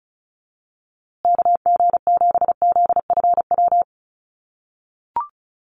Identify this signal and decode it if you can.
Morse code, heard as a keyed tone, about 34 wpm, 700 Hz. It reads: KG7ZFW
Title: USA Call Signs Repeated 2x 35wpm